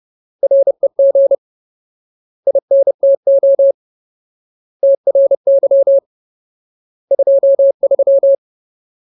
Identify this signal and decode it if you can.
Morse code, heard as a keyed tone, about 30 wpm, 550 Hz. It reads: REG INTO TRY 23
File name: Ultimate Word List / 30WPM Ultimate